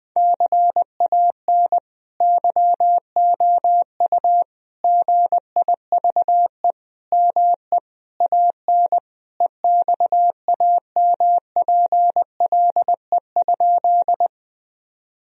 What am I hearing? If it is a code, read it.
Morse code, heard as a keyed tone, about 20 wpm, 700 Hz. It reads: CAN YOU GIVE ME AN EXAMPLE?
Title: Sentences from Top 1000 Words - Review Part 1 - 20wpm